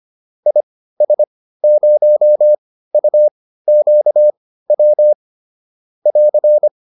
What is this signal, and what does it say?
Morse code, heard as a keyed tone, about 25 wpm, spaced as for 15 wpm, 600 Hz. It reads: IS0UQW <AR>
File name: QSO409 15